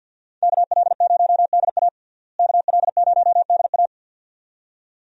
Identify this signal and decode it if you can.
Morse code, heard as a keyed tone, about 49 wpm, 700 Hz. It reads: KL0DA KL0DA
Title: USA Call Signs - 5 Characters - Repeated 2x - 50wpm